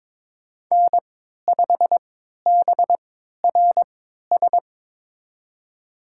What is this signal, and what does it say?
Morse code, heard as a keyed tone, about 22 wpm, spaced as for 13 wpm, 700 Hz. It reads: N5BRS
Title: USA Call Signs - 5 Characters - 3x Character Spacing - 22wpm